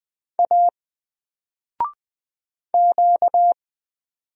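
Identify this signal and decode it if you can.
Morse code, heard as a keyed tone, about 20 wpm, 700 Hz. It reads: A Q
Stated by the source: Single letter and number Morse drill at 20 wpm, from N4LCD.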